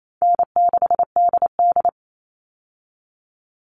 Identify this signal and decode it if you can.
Morse code, heard as a keyed tone, about 28 wpm, 700 Hz. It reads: N6DD
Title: All Cut Numbers - 4-Digit 28wpm